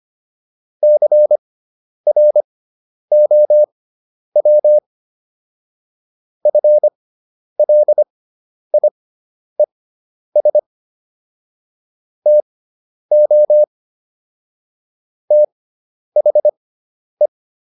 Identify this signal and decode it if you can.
Morse code, heard as a keyed tone, about 25 wpm, spaced as for 10 wpm, 600 Hz. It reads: CROW FLIES TO THE